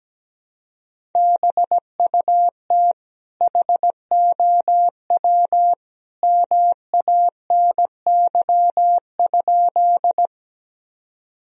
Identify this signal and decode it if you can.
Morse code, heard as a keyed tone, about 17 wpm, 700 Hz. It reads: BUT HOW MANY?
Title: Sentences from Top 1000 Words - Review Part 1 - 17wpm